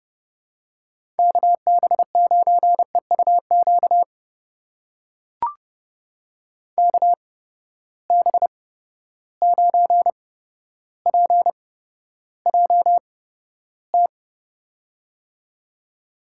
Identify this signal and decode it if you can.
Morse code, heard as a keyed tone, about 30 wpm, spaced as for 13 wpm, 700 Hz. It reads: KB9EUQ KB9PJT
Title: USA Call Signs - 6 Characters - 8x Character Spacing - 30wpm